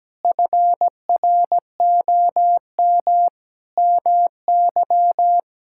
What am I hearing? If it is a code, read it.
Morse code, heard as a keyed tone, about 17 wpm, 700 Hz. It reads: FROM MY
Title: Sentences from Top 1000 Words - Review Part 6 - 17wpm